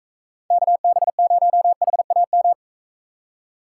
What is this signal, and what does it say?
Morse code, heard as a keyed tone, about 44 wpm, 700 Hz. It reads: KB0HAM